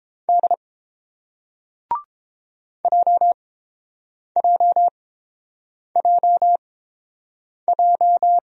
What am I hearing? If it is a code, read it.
Morse code, heard as a keyed tone, about 30 wpm, spaced as for 7 wpm, 700 Hz. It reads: D JJJJ